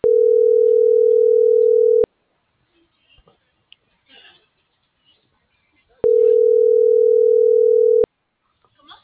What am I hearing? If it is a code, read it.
Ringback tone (North America), 440 + 480 Hz.